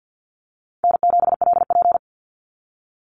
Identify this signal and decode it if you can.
Morse code, heard as a keyed tone, about 49 wpm, 700 Hz. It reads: N7LP
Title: NA SOTA Chaser Callsigns 50wpm